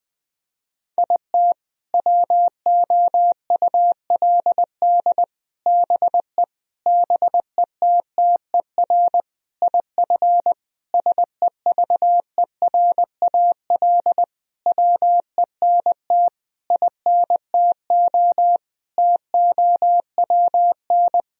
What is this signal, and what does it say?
Morse code, heard as a keyed tone, about 21 wpm, 700 Hz. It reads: IT WOULD BE BETTER IF SEVERAL WENT INTO TOWN
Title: Sentences from Top 1000 Words - Review Part 2 - 20wpm